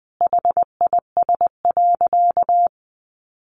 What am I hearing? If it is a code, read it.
Morse code, heard as a keyed tone, about 20 wpm, 700 Hz. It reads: HIS.
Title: Sentences from Top 1000 Words 20wpm